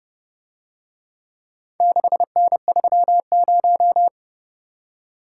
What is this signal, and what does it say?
Morse code, heard as a keyed tone, about 29 wpm, 700 Hz. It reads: 6N30